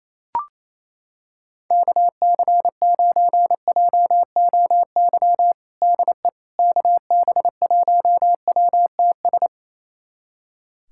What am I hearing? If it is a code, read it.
Morse code, heard as a keyed tone, about 27 wpm, 700 Hz. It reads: KC9JOY DE KB1WTS
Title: USA Callsign DE Callsign - 6 Characters - 28wpm